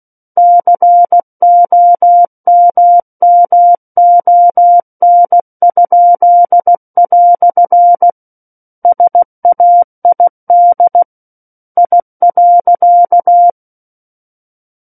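Morse code, heard as a keyed tone, about 16 wpm, 700 Hz. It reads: COMMON?" SAID I.
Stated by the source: Morse practice of book text at 16 wpm.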